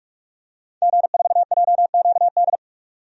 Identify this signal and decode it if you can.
Morse code, heard as a keyed tone, about 46 wpm, 700 Hz. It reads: G4JQD
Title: DXCC-100 Call Signs - 5 Characters - 45wpm